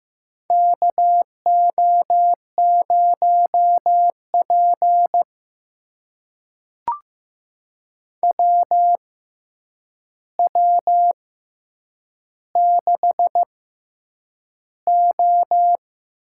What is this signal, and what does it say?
Morse code, heard as a keyed tone, about 15 wpm, spaced as for 8 wpm, 700 Hz. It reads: KO0P WW6O
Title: USA Call Signs - 4 Characters - 6x Character Spacing - 15wpm